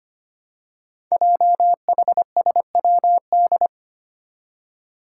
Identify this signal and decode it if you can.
Morse code, heard as a keyed tone, about 25 wpm, 700 Hz. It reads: JHSWD
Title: Code Groups - 5 Letters - 25wpm